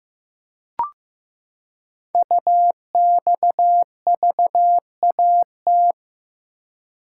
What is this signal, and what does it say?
Morse code, heard as a keyed tone, about 15 wpm, 700 Hz. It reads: UXVAT
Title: Code Groups - 5 Letters - 15wpm